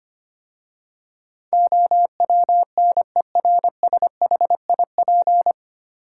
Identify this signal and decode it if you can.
Morse code, heard as a keyed tone, about 25 wpm, 700 Hz. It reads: OWNERSHIP
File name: practice / 25wpm / 9 Letter Words - First 3 Letters Given - 25wpm